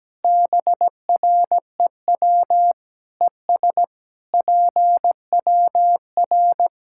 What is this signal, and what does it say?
Morse code, heard as a keyed tone, about 17 wpm, 700 Hz. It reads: BREW ES PWR
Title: Practice QSO - Exchange 2 - Info - 17wpm